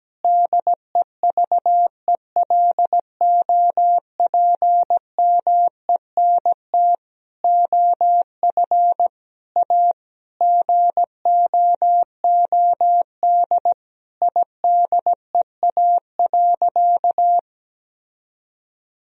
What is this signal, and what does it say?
Morse code, heard as a keyed tone, about 17 wpm, 700 Hz. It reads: DEVELOPMENT OF A GOOD IDEA.
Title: Sentences from Top 1000 Words - Review Part 4 - 17wpm